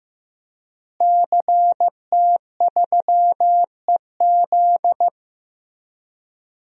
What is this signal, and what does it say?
Morse code, heard as a keyed tone, about 15 wpm, 700 Hz. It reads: CT3EZ
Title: DXCC-100 Call Signs - 5 Characters - Repeated 3x - 15wpm